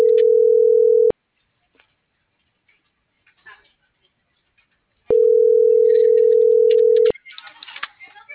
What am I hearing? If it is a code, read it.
Ringback tone (North America), 440 + 480 Hz.